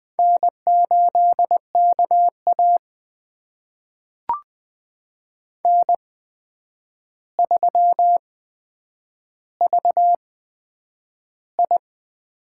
Morse code, heard as a keyed tone, about 20 wpm, spaced as for 9 wpm, 700 Hz. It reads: N8KA N3VI
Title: USA Call Signs - 4 Characters - 8x Character Spacing - 20wpm